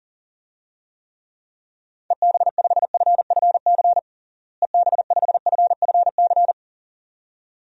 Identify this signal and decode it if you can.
Morse code, heard as a keyed tone, about 40 wpm, 700 Hz. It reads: EB5FFC EB5FFC
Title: DXCC-100 Call Signs - 6 Characters - Repeated 2x - 40wpm